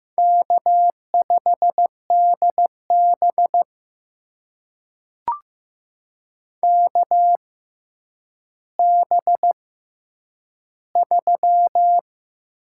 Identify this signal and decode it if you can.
Morse code, heard as a keyed tone, about 15 wpm, spaced as for 9 wpm, 700 Hz. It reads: K5DB KB3
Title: USA Call Signs - 6x Character Spacing - 15wpm